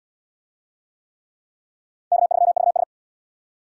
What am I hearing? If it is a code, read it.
Morse code, heard as a keyed tone, about 74 wpm, 700 Hz. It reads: K3VS